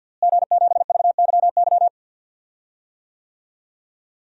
Morse code, heard as a keyed tone, about 53 wpm, 700 Hz. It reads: G7VYY
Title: DXCC-100 Call Signs - 5 Characters - Repeated 2x - 50wpm